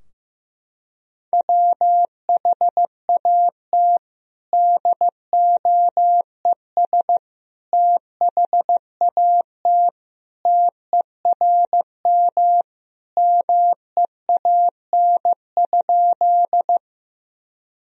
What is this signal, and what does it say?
Morse code, heard as a keyed tone, about 16 wpm, 700 Hz. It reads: WHAT DOES THAT TERM MEAN?
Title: Sentences from Top 1000 Words - Review Part 4 - 15wpm